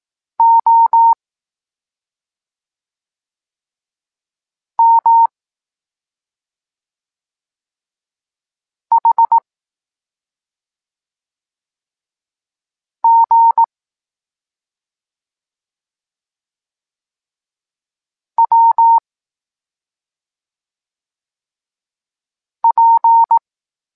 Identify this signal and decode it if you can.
Morse code, heard as a keyed tone, about 18 wpm, 920 Hz. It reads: O M H G W P
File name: Tekst-26 1